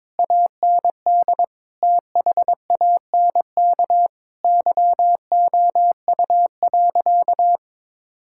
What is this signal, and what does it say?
Morse code, heard as a keyed tone, about 22 wpm, 700 Hz. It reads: AND THANK YOU.
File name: practice / 22wpm / Easy Sentences with Hints - 22wpm